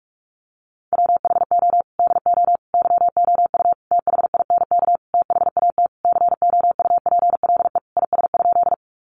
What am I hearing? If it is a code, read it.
Morse code, heard as a keyed tone, about 44 wpm, 700 Hz. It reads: WHO DO YOU THINK THAT COUPLE IS?
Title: Sentences from Top 1100 Words 45wpm